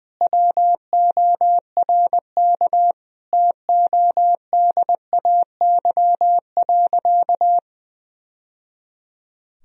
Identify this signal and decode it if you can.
Morse code, heard as a keyed tone, about 20 wpm, 700 Hz. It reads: WORK TODAY.
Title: Sentences from Top 1000 Words - Review Part 4 - 20wpm